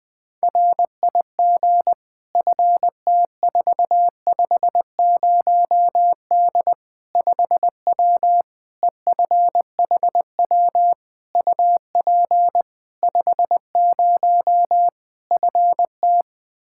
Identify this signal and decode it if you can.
Morse code, heard as a keyed tone, about 21 wpm, 700 Hz. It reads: RIG FT450D 5W EFHW UP 50 FT
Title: Practice QSO - Exchange 2 - Info - 20wpm